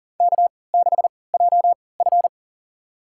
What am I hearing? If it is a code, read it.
Morse code, heard as a keyed tone, about 40 wpm, spaced as for 23 wpm, 700 Hz. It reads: K6JF
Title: USA Call Signs - 4 Characters - 3x Character Spacing - 40wpm